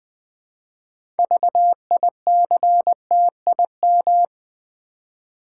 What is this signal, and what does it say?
Morse code, heard as a keyed tone, about 20 wpm, 700 Hz. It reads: VICTIM
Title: 6 Letter Words - First 3 Letters Given - 20wpm